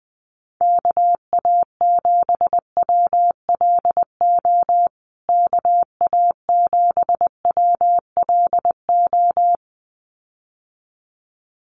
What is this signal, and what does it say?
Morse code, heard as a keyed tone, about 20 wpm, 700 Hz. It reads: KA7WLO KA7WLO
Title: USA Call Signs Repeated 2x 20wpm